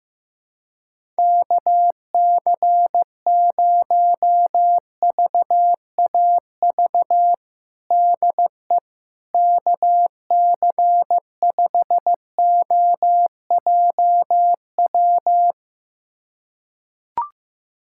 Morse code, heard as a keyed tone, about 16 wpm, 700 Hz. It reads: KC0VAV DE KC5OJW
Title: USA Callsign DE Callsign 15wpm